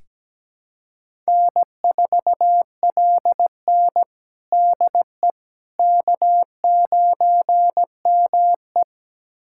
Morse code, heard as a keyed tone, about 17 wpm, 700 Hz. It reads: N4LN DE K9ME
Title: USA Callsign DE Callsign - 4 Characters - 17wpm